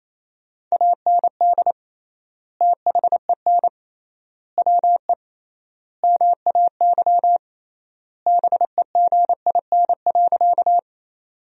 Morse code, heard as a keyed tone, about 28 wpm, 700 Hz. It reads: AND THEN WE MAY BEGIN.
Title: 3x Word-Spacing - Sentences from Top 300 Words - 28wpm